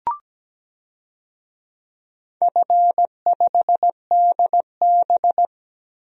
Morse code, heard as a keyed tone, about 17 wpm, 700 Hz. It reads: F5DB
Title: DXCC-100 Call Signs - 4 Characters - Repeated 2x - 17wpm